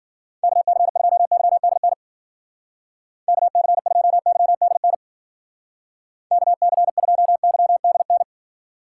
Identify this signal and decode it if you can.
Morse code, heard as a keyed tone, about 49 wpm, 700 Hz. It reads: KC2YDN KC2YDN KC2YDN
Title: Speed Racing - USA 2x3 Call Signs 40wpm